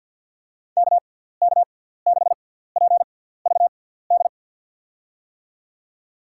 Morse code, heard as a keyed tone, about 53 wpm, spaced as for 17 wpm, 700 Hz. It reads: KK6PVD
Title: USA Call Signs - 6 Characters - 6x Character Spacing - 50wpm